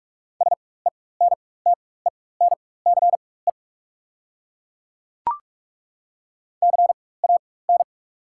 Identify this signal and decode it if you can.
Morse code, heard as a keyed tone, about 42 wpm, spaced as for 21 wpm, 690 Hz. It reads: SENTENCE CAN
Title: 4x Character Spacing - Top 200 Words - 45wpm